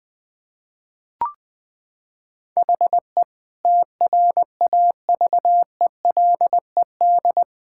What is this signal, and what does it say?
Morse code, heard as a keyed tone, about 20 wpm, 700 Hz. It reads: HE TRAVELED
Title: Sentences with 6-Word Phrases - 20wpm